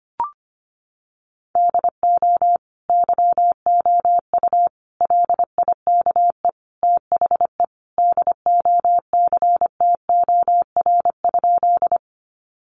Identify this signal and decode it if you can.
Morse code, heard as a keyed tone, about 25 wpm, 700 Hz. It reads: DO YOU LIKE THE DOCTOR?
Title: Sentences from Top 1500 Words 25wpm